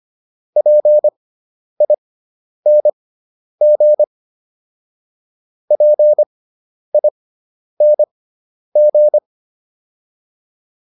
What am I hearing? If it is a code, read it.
Morse code, heard as a keyed tone, about 25 wpm, spaced as for 10 wpm, 600 Hz. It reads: PING PING